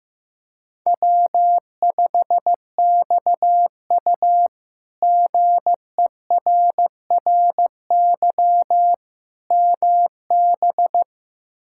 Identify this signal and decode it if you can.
Morse code, heard as a keyed tone, about 15 wpm, 700 Hz. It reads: W5XU GERRY MB
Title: K1USN SST QSO Practice - 15wpm